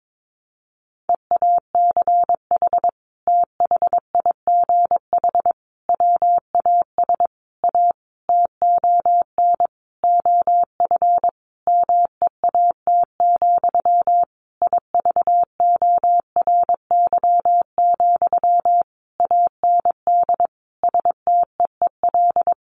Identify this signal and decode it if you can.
Morse code, heard as a keyed tone, about 22 wpm, 700 Hz. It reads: EACH THIGH WAS A TON OF MEAT, IVORY, AND STEEL
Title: A Sound Of Thunder 22wpm